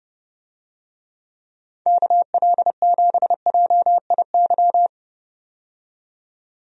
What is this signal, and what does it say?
Morse code, heard as a keyed tone, about 30 wpm, 700 Hz. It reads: KL7JIY
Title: USA Call Signs - 6 Characters - 30wpm